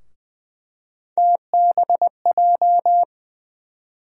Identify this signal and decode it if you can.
Morse code, heard as a keyed tone, about 21 wpm, 700 Hz. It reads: TBJ
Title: USA Call Sign Postfixes - 3 Letters 20wpm